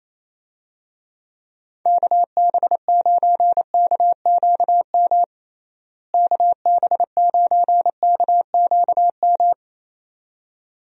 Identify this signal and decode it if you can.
Morse code, heard as a keyed tone, about 28 wpm, 700 Hz. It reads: KB9KQM KB9KQM
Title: USA Call Signs - 6 Characters - Repeated 2x - 28wpm